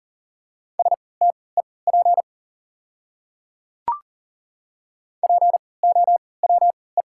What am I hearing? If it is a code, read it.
Morse code, heard as a keyed tone, about 38 wpm, spaced as for 22 wpm, 700 Hz. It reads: STEP POWE
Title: 3x Character Spacing - Top 500 Words - 40wpm